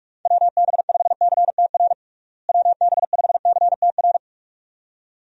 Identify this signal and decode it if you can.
Morse code, heard as a keyed tone, about 46 wpm, 700 Hz. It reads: WB5CTR WB5CTR
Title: USA Call Signs - 6 Characters - Repeated 2x - 45wpm